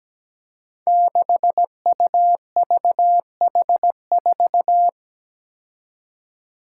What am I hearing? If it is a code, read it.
Morse code, heard as a keyed tone, about 17 wpm, 700 Hz. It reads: 6UVH4
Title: Five Challenging Letter-Number Combinations 17wpm